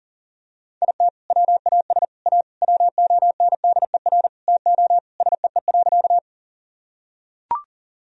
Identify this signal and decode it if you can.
Morse code, heard as a keyed tone, about 40 wpm, 700 Hz. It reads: IT WAS A WONDER TO SEE.
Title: Sentences from Top 1000 Words - Review Part 3 - 40wpm